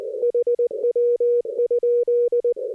Morse code, heard as a keyed tone, about 20 wpm, 480 Hz. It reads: HW?
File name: 60m qso